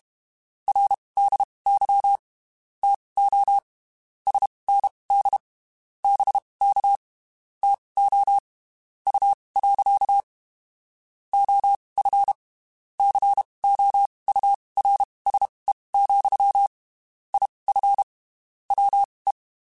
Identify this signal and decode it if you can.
Morse code, heard as a keyed tone, about 32 wpm, spaced as for 23 wpm, 800 Hz. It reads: RDY TO SND BK TO U. OF COURSE, IF WE
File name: (spaced) / QSOs Spaced-mp3s / WA7QOX 1 @30 sp